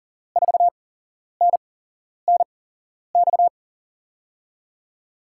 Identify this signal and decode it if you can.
Morse code, heard as a keyed tone, about 40 wpm, spaced as for 11 wpm, 700 Hz. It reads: 4NNX
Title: USA Call Signs - 6 Characters - 8x Character Spacing - 40wpm